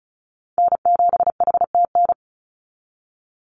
Morse code, heard as a keyed tone, about 34 wpm, 700 Hz. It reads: N7HTN
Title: USA Call Signs 35wpm